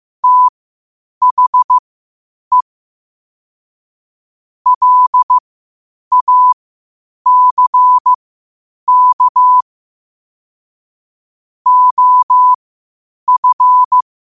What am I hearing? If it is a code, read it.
Morse code, heard as a keyed tone, about 15 wpm, spaced as for 8 wpm, 1000 Hz. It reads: THE LACK OF
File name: qotdfiles / 8wpm